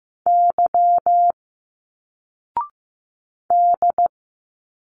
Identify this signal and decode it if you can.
Morse code, heard as a keyed tone, about 15 wpm, 700 Hz. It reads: Y D
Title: Single Letters 15wpm